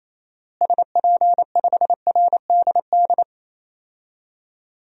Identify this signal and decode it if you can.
Morse code, heard as a keyed tone, about 28 wpm, 700 Hz. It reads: SP5RDD
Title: DXCC-100 Call Signs - 6 Characters - Repeated 2x - 28wpm